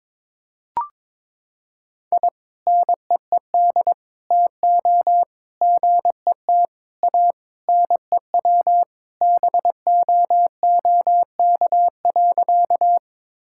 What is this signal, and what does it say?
Morse code, heard as a keyed tone, about 22 wpm, 700 Hz. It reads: I NEED TO GET A NEW BOOK.
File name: practice / 22wpm / Sentences with 3-Word Phrases - 22wpm